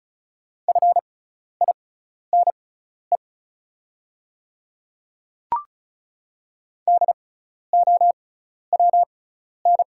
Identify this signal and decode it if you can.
Morse code, heard as a keyed tone, about 34 wpm, spaced as for 12 wpm, 690 Hz. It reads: FINE DOWN
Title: 6x Character Spacing - Top 500 Words - 35wpm